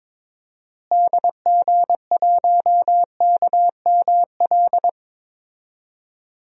Morse code, heard as a keyed tone, about 22 wpm, 700 Hz. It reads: DG1KML
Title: DXCC-100 Call Signs - 6 Characters - Repeated 3x - 22wpm